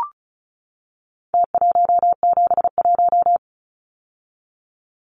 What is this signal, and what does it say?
Morse code, heard as a keyed tone, about 34 wpm, 700 Hz. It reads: T171
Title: Numbers - Contest Style - 4-Digit 35wpm